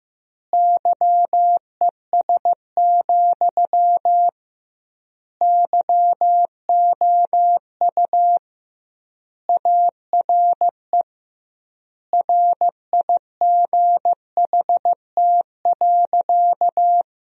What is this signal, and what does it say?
Morse code, heard as a keyed tone, about 15 wpm, 700 Hz. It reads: YES, YOU ARE RIGHT.
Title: 2x Word-Spacing - Sentences from Top 500 Words - 15wpm